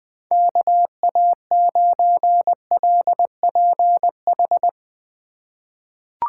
Morse code, heard as a keyed tone, about 20 wpm, 700 Hz. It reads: KA9LPH
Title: USA Call Signs - 6 Characters - 2x Character Spacing - 20wpm